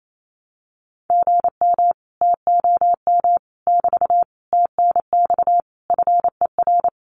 Morse code, heard as a keyed tone, about 28 wpm, 700 Hz. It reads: GM TOM <BT> TNX FER
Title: Common QSO Phrases Encore 28wpm